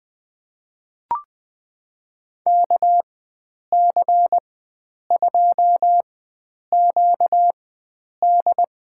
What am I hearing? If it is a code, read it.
Morse code, heard as a keyed tone, about 20 wpm, spaced as for 9 wpm, 700 Hz. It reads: KC2QD
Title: USA Call Signs - 4x Character Spacing - 20wpm